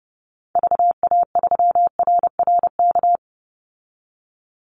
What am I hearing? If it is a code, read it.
Morse code, heard as a keyed tone, about 30 wpm, 700 Hz. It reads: VA3RRK
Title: Call Signs Repeated 3x - 30wpm